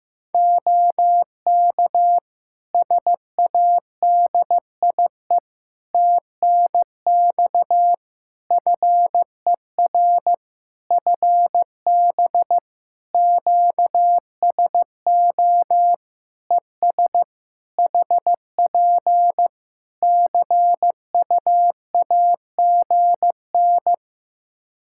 Morse code, heard as a keyed tone, about 15 wpm, 700 Hz. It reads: OK SADIE TNX FER FB QSO ES HP CUAGN
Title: Practice QSO - Exchange 4 - Ending - 15wpm